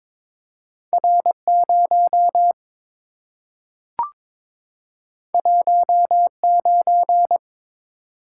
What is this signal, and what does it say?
Morse code, heard as a keyed tone, about 23 wpm, 690 Hz. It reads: R0 19
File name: practice / 22wpm / Two Letter-Number Combinations - 22wpm